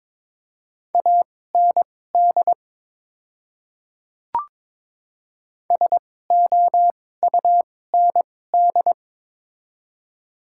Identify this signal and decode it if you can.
Morse code, heard as a keyed tone, about 22 wpm, spaced as for 16 wpm, 700 Hz. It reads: AND SOUND
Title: 2x Character Spacing - Top 200 Words - 22wpm